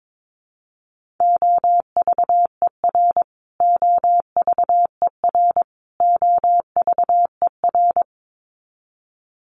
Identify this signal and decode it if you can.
Morse code, heard as a keyed tone, about 22 wpm, 700 Hz. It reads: OVER OVER OVER
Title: Top 300 Words Repeated 22wpm